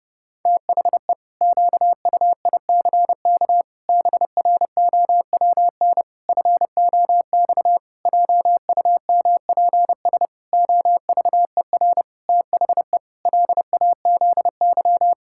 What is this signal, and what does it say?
Morse code, heard as a keyed tone, about 32 wpm, 700 Hz. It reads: THE QUICK BROWN FOX JUMPS OVER THE LAZY
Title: Pangram Challenge v2 - 30wpm